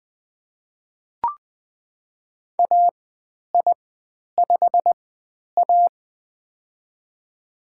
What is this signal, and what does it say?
Morse code, heard as a keyed tone, about 20 wpm, spaced as for 10 wpm, 700 Hz. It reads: AI5A
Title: 102 Lesson 17g - Letter u - Call Signs 20wpm